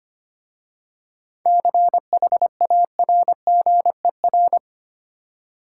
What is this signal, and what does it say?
Morse code, heard as a keyed tone, about 25 wpm, 700 Hz. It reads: CHARGER
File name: practice / 25wpm / Easy Words with Hints - 25wpm